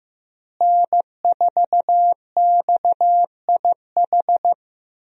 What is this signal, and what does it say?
Morse code, heard as a keyed tone, about 15 wpm, 700 Hz. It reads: N4XIH
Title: USA Call Signs - 5 Characters - Repeated 2x - 15wpm